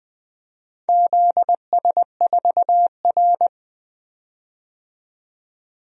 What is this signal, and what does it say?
Morse code, heard as a keyed tone, about 21 wpm, 700 Hz. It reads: ZS4R
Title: DXCC-100 Call Signs - 4 Characters - 20wpm